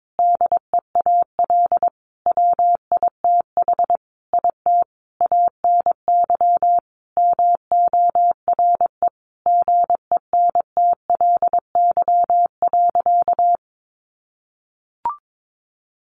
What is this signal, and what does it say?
Morse code, heard as a keyed tone, about 22 wpm, 700 Hz. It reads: DEAL WITH IT ANY MORE GENTLY.
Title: Sentences from Top 1500 Words 22wpm